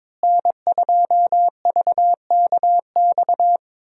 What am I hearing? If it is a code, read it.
Morse code, heard as a keyed tone, about 21 wpm, 700 Hz. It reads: N2VKX
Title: USA Call Signs - 5 Characters - Repeated 2x - 22wpm